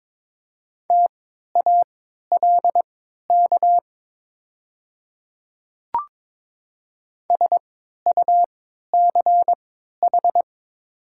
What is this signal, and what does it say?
Morse code, heard as a keyed tone, about 22 wpm, spaced as for 12 wpm, 700 Hz. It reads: TALK SUCH
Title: 3x Character Spacing - Top 500 Words - 22wpm